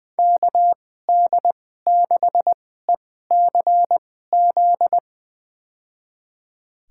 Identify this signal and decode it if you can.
Morse code, heard as a keyed tone, about 20 wpm, spaced as for 14 wpm, 700 Hz. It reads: KD6ECZ
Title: USA Call Signs - 6 Characters - 2x Character Spacing - 20wpm